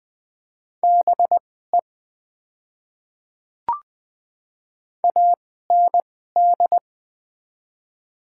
Morse code, heard as a keyed tone, about 20 wpm, spaced as for 14 wpm, 700 Hz. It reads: BE AND
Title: 2x Character Spacing - Top 100 Words - 20wpm